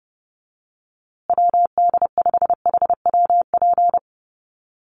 Morse code, heard as a keyed tone, about 30 wpm, 700 Hz. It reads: WD5HWP
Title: USA Call Signs Repeated 2x 30wpm